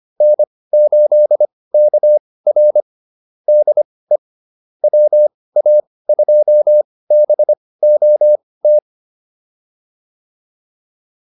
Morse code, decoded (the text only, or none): N8KR DE WA2BOT